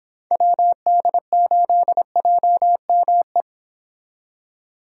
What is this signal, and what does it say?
Morse code, heard as a keyed tone, about 30 wpm, 700 Hz. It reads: WD8JME